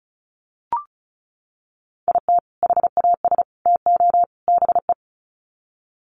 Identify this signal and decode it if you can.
Morse code, heard as a keyed tone, about 34 wpm, 700 Hz. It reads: IT HAS TO BE
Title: Sets of 4 Words 35wpm